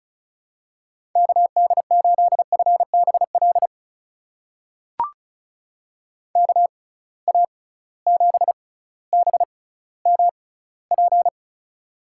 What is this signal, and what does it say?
Morse code, heard as a keyed tone, about 36 wpm, spaced as for 18 wpm, 700 Hz. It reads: KD8FBL KA7BMP
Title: USA Call Signs - 6 Characters - 6x Character Spacing - 35wpm